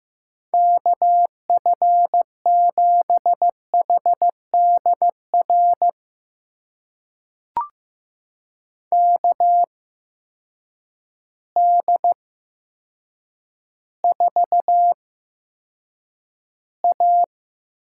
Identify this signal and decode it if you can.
Morse code, heard as a keyed tone, about 15 wpm, spaced as for 8 wpm, 700 Hz. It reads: KF7HDR KD4A
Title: USA Call Signs - 6 Characters - 8x Character Spacing - 15wpm